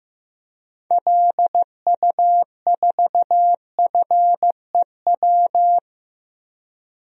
Morse code, heard as a keyed tone, about 15 wpm, 700 Hz. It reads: LU4FEW
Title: DXCC-100 Call Signs - 6 Characters - 15wpm